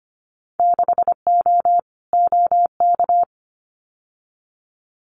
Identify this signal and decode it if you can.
Morse code, heard as a keyed tone, about 25 wpm, 690 Hz. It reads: 6O OK
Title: Winter Field Day Exchanges Encore 25wpm